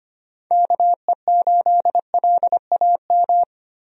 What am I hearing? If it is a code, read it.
Morse code, heard as a keyed tone, about 25 wpm, 700 Hz. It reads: KE8LAM
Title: USA Call Signs - 6 Characters - 25wpm